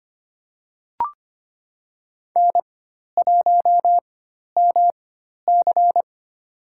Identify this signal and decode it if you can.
Morse code, heard as a keyed tone, about 25 wpm, spaced as for 12 wpm, 700 Hz. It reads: N1MC
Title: USA Call Signs - 4 Characters - 4x Character Spacing - 25wpm